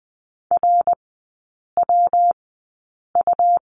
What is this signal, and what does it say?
Morse code, heard as a keyed tone, about 20 wpm, spaced as for 8 wpm, 700 Hz. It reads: RWU
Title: 114 Lesson 19e - Letter w - Three Characters 20wpm